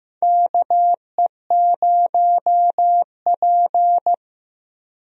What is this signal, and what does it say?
Morse code, heard as a keyed tone, about 15 wpm, 700 Hz. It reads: KE0P
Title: USA Call Signs - 4 Characters - Repeated 2x - 15wpm